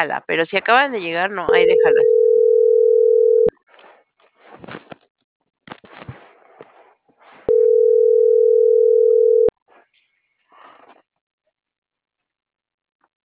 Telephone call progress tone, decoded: ringback tone

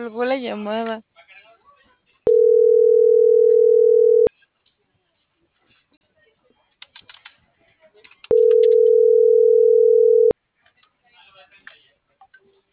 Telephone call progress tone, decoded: ringback tone

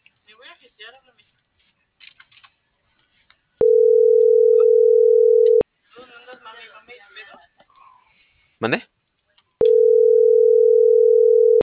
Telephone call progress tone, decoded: ringback tone

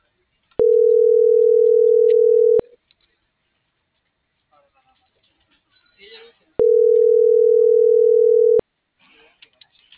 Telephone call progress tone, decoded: ringback tone

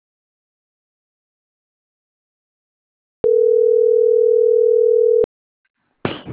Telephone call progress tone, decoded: ringback tone

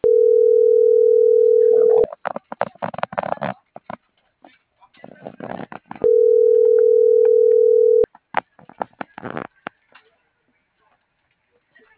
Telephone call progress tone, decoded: ringback tone

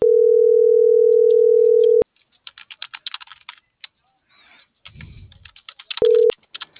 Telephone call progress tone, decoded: ringback tone

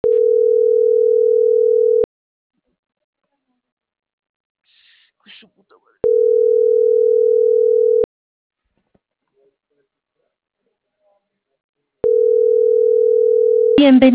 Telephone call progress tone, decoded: ringback tone

